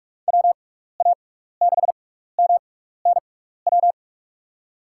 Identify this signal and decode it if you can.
Morse code, heard as a keyed tone, about 46 wpm, spaced as for 15 wpm, 700 Hz. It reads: WA6MNW
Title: USA Call Signs - 6x Character Spacing - 45wpm